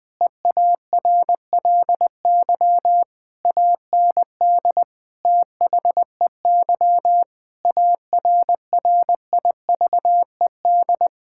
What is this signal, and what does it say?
Morse code, heard as a keyed tone, about 21 wpm, 700 Hz. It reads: EARLY AND THEY ARRIVED
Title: Sentences with 2-Word Phrases - 20wpm